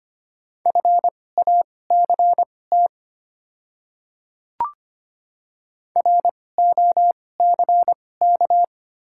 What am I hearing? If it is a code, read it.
Morse code, heard as a keyed tone, about 25 wpm, spaced as for 18 wpm, 700 Hz. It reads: FACT ROCK
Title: 2x Character Spacing - Top 500 Words - 25wpm